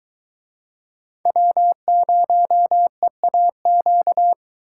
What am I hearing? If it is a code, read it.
Morse code, heard as a keyed tone, about 22 wpm, 700 Hz. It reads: W0EAQ